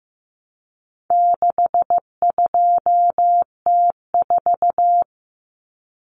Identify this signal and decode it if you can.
Morse code, heard as a keyed tone, about 15 wpm, 700 Hz. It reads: 62T4
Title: Common Cut Numbers - 4-Digit 15wpm